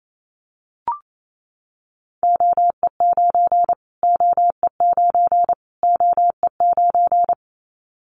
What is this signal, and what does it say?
Morse code, heard as a keyed tone, about 28 wpm, 700 Hz. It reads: OE9 OE9 OE9
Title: Callsign Prefixes for Top 25 DXCC - Repeated 3x - 28wpm